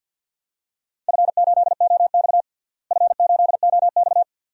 Morse code, heard as a keyed tone, about 47 wpm, 700 Hz. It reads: F8OX F8OX
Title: DXCC-100 Call Signs - 4 Characters - Repeated 2x - 50wpm